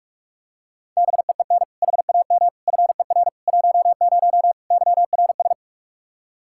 Morse code, heard as a keyed tone, about 41 wpm, 700 Hz. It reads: BEEN HAM FER 10 YRS
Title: Practice QSO - Exchange 3 - More Info - 45wpm